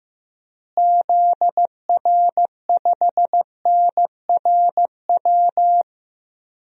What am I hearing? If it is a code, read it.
Morse code, heard as a keyed tone, about 15 wpm, 700 Hz. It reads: ZR5NRW
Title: DXCC-100 Call Signs - 6 Characters - Repeated 3x - 15wpm